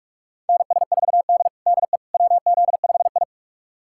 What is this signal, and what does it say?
Morse code, heard as a keyed tone, about 47 wpm, 700 Hz. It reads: NS4D DE WZ5I
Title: USA Callsign DE Callsign - 4 Characters - 45wpm